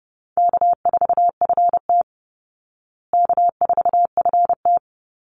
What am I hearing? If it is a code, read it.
Morse code, heard as a keyed tone, about 30 wpm, 700 Hz. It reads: K4FT K4FT
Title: Easy Call Signs Repeated 3x - 30wpm